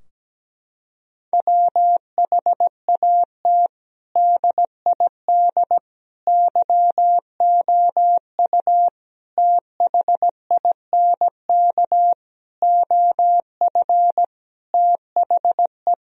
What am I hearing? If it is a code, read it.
Morse code, heard as a keyed tone, about 17 wpm, 700 Hz. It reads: WHAT DID YOU THINK OF THE
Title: Sentences from Top 1000 Words - Review Part 4 - 17wpm